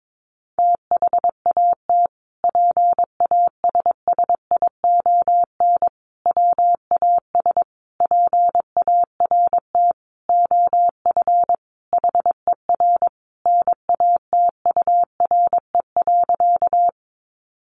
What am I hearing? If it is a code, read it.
Morse code, heard as a keyed tone, about 22 wpm, 700 Hz. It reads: THAT PASSION WAS PART OF HER NATURE.
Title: Sentences from Top 1300 Words 22wpm